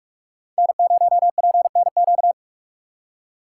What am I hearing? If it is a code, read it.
Morse code, heard as a keyed tone, about 43 wpm, 700 Hz. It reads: N0PNQ